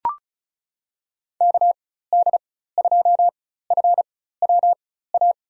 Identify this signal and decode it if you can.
Morse code, heard as a keyed tone, about 36 wpm, spaced as for 16 wpm, 700 Hz. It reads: KD2FWA
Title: USA Call Signs - 6 Characters - 4x Character Spacing - 35wpm